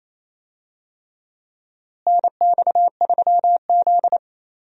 Morse code, heard as a keyed tone, about 27 wpm, 700 Hz. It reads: NX3Z
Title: USA Call Signs - 4 Characters - Repeated 3x - 28wpm